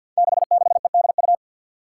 Morse code, heard as a keyed tone, about 52 wpm, 700 Hz. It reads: 66EDU
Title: Five Challenging Letter-Number Combinations 50wpm